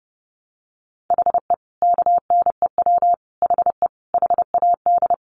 Morse code, heard as a keyed tone, about 30 wpm, 700 Hz. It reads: HE KNEW HE HAD
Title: Sentences from Top 600 Words 30wpm